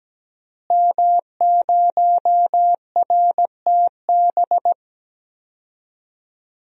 Morse code, heard as a keyed tone, about 17 wpm, 700 Hz. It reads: M0RTB
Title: DXCC-100 Call Signs - 5 Characters - Repeated 2x - 17wpm